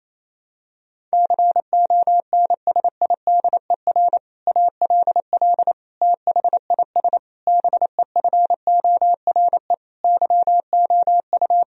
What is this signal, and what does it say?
Morse code, heard as a keyed tone, about 28 wpm, 700 Hz. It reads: CONSIDER ALL THIS BEFORE YOU
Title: Sentences from Top 1000 Words - Review Part 4 - 28wpm